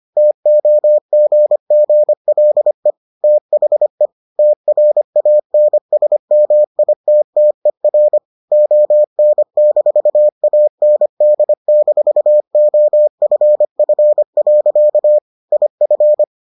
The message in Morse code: TOGGLE THE TRANSMITTER ON-AND-OFF. IF

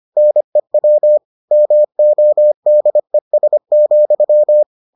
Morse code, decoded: NEW MODES,

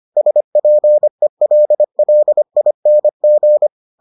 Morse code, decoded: SPELLING

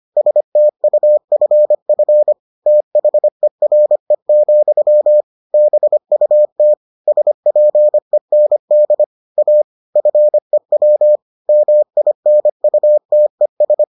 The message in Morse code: STUFF THERE, BUT SPEND A FEW MINUTES

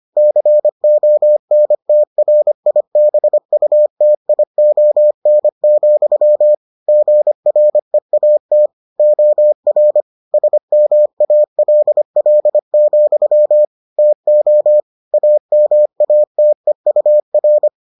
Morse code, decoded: CONTRIBUTION, GREAT OR SMALL, TO AMATEUR

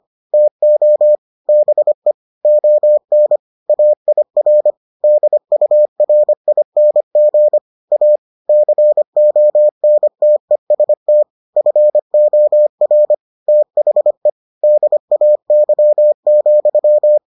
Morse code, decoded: TO BE ON AIR DURING A CONTEST FOR THE DAY,

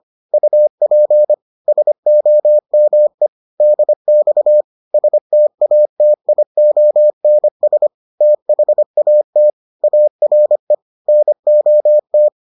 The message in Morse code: UP SOME DX STATIONS THAT ARE NOT

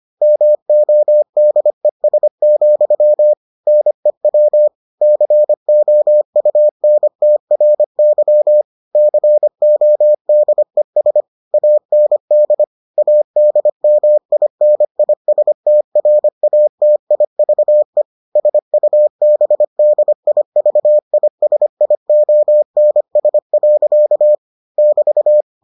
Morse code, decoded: MODES, NEW COUNTRY CODES AND ADMINISTRATIVE SUBDIVISIONS. <BT>

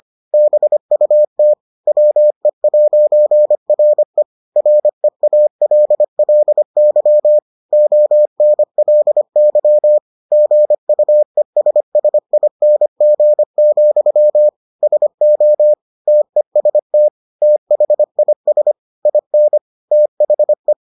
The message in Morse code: BUT WE'RE REALLY ONLY GUESSING, SO TEST THIS IN THE

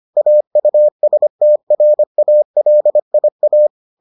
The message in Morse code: AUSTRALIA

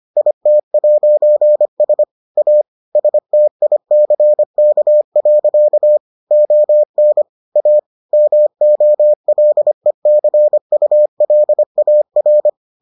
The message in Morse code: IT'S A STICK. ON A MOLECULAR

